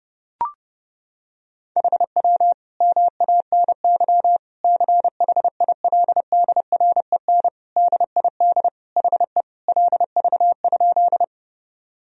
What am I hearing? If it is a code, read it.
Morse code, heard as a keyed tone, about 30 wpm, 700 Hz. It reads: HW MANY CHILDREN DID HE LV?
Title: Sentences from Top 400 Words and Abbreviations 30wpm